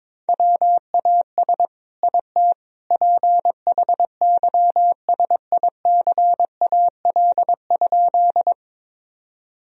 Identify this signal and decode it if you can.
Morse code, heard as a keyed tone, about 22 wpm, 700 Hz. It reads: WAS IT PHYSICAL?
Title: Sentences from Top 1000 Words - Review Part 4 - 22wpm